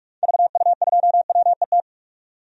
Morse code, heard as a keyed tone, about 46 wpm, 700 Hz. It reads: VU2WET
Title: DXCC-100 Call Signs - 6 Characters - 45wpm